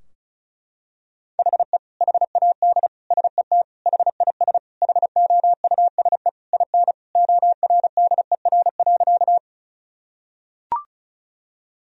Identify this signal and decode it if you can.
Morse code, heard as a keyed tone, about 34 wpm, 700 Hz. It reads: HE HAD SET HIS HOUSE IN ORDER.
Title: Sentences from Top 1000 Words - Review Part 2 - 35wpm